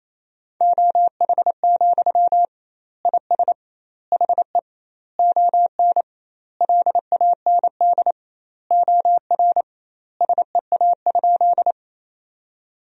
Morse code, heard as a keyed tone, about 28 wpm, 700 Hz. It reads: OH, IS HE ON LAND OR SEA?
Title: 2x Word-Spacing - Sentences from Top 500 Words - 28wpm